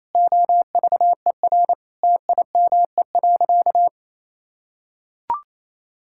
Morse code, heard as a keyed tone, about 28 wpm, 700 Hz. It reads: OVER TIME.
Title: Sentences from Top 1000 Words - Review Part 6 - 28wpm